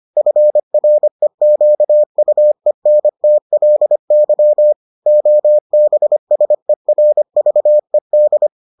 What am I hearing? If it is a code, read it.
Morse code, heard as a keyed tone, about 25 wpm, 600 Hz. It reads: FREQUENTLY OBSERVED